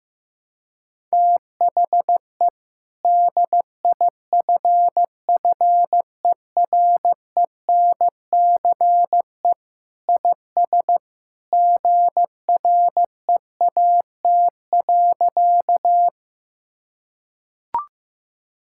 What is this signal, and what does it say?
Morse code, heard as a keyed tone, about 15 wpm, 700 Hz. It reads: THE DIFFERENCE IS GREAT.
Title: Sentences from Top 1000 Words - Review Part 5 - 15wpm